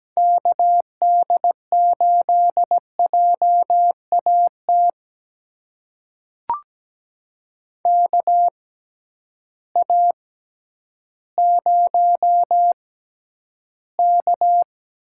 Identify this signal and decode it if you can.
Morse code, heard as a keyed tone, about 17 wpm, spaced as for 10 wpm, 700 Hz. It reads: KD8JAT KA0K
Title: USA Call Signs - 6 Characters - 6x Character Spacing - 17wpm